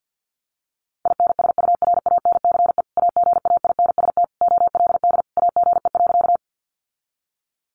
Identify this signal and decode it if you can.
Morse code, heard as a keyed tone, about 50 wpm, 700 Hz. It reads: INSURANCE AGAINST OLD AGE.
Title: Sentences from Top 1100 Words 50wpm